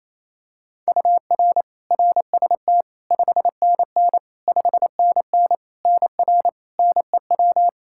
Morse code, heard as a keyed tone, about 28 wpm, 700 Hz. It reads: UR RST 5NN 5NN NR NEW
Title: Practice QSO - Exchange 1 - Beginning - 28wpm